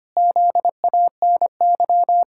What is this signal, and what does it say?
Morse code, heard as a keyed tone, about 25 wpm, 700 Hz. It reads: ZANY
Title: Pangram Challenge v2 - 25wpm